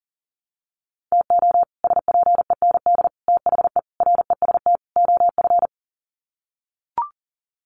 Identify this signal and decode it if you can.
Morse code, heard as a keyed tone, about 37 wpm, 700 Hz. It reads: TO SPEND THE REST OF
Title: Sets of 5 Words Encore2 40wpm